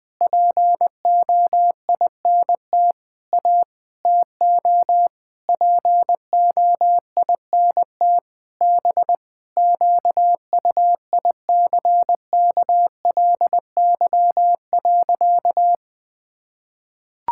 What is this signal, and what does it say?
Morse code, heard as a keyed tone, about 21 wpm, 700 Hz. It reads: POINT A TO POINT B QUICKLY.
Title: Sentences with 6-Word Phrases - 20wpm